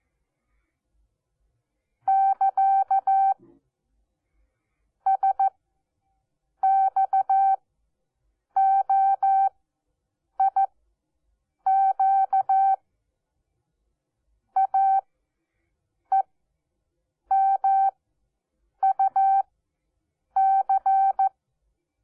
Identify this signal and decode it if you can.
Morse code, heard as a keyed tone, about 14 wpm, spaced as for 7 wpm, 780 Hz. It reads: <KA> SXOIQ AEMUC
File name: Onlinelehrgang / morsen / DJ4UF Morsen12